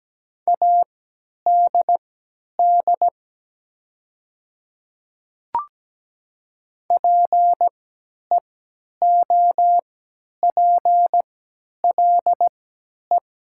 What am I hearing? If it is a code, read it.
Morse code, heard as a keyed tone, about 17 wpm, spaced as for 10 wpm, 700 Hz. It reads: ADD PEOPLE
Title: 3x Character Spacing - Top 200 Words - 17wpm